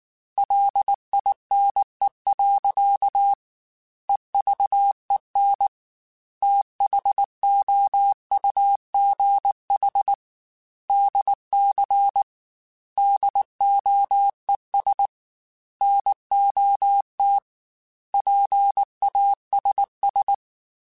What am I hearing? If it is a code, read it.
Morse code, heard as a keyed tone, about 19 wpm, 800 Hz. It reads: LINE. EVEN THOUGH DC DOES NOT PASS
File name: Technical Subjects / Technical Subjects-mp3s / Sw Pwr Sup @18